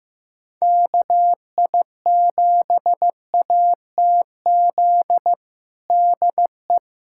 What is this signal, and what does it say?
Morse code, heard as a keyed tone, about 15 wpm, 700 Hz. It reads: KI7ATZ DE
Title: USA Callsign DE Callsign - 6 Characters - 15wpm